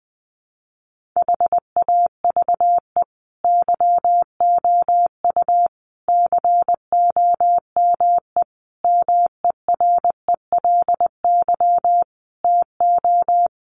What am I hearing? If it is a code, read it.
Morse code, heard as a keyed tone, about 19 wpm, 700 Hz. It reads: HAVE YOU COME MERELY TO
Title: Sentences from Top 1400 Words 20wpm